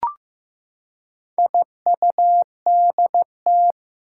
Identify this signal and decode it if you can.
Morse code, heard as a keyed tone, about 15 wpm, 700 Hz. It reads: IUDT
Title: Four Challenging Letter-Number Combinations 15wpm